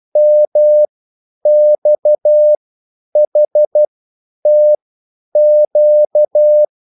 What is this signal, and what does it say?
Morse code, heard as a keyed tone, about 12 wpm, spaced as for 9 wpm, 600 Hz. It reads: MXHTQ